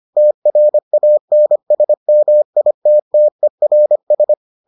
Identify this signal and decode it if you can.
Morse code, heard as a keyed tone, about 25 wpm, 600 Hz. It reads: TRANSMITTERS